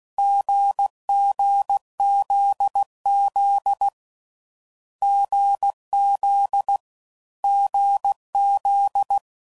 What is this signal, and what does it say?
Morse code, heard as a keyed tone, about 16 wpm, 800 Hz. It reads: GGZZ GZ GZ
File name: Combinations / Difficult Combos / GZ GZ7